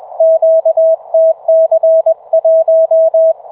Morse code, heard as a keyed tone, about 24 wpm, 660 Hz. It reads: QTC1